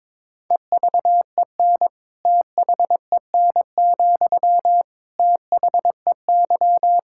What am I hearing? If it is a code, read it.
Morse code, heard as a keyed tone, about 21 wpm, 700 Hz. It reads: EVEN THEN, THEY
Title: Sentences from Top 1600 Words 22wpm